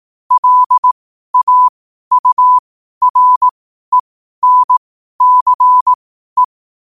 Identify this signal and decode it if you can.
Morse code, heard as a keyed tone, about 18 wpm, spaced as for 12 wpm, 1000 Hz. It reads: LAURENCE
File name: qotdfiles / 12wpm